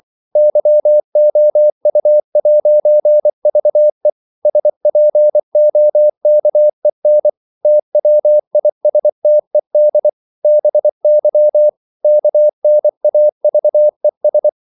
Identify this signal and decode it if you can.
Morse code, heard as a keyed tone, about 24 wpm, 600 Hz. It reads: YOU'VE SPOKEN TWISTED BY KNAVES
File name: CW / Practice / If24